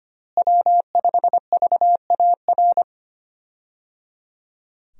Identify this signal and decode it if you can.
Morse code, heard as a keyed tone, about 25 wpm, 700 Hz. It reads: W5VAR
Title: USA Call Signs - 5 Characters - Repeated 3x - 25wpm